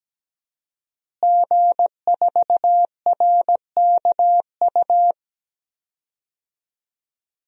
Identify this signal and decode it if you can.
Morse code, heard as a keyed tone, about 17 wpm, 700 Hz. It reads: G4RKU
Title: DXCC-100 Call Signs - 5 Characters - Repeated 2x - 17wpm